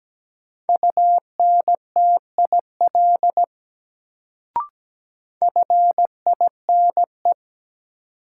Morse code, heard as a keyed tone, about 17 wpm, 700 Hz. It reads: UNTIL FINE
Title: Sentences from Top 500 Words - New Word Drill - 17wpm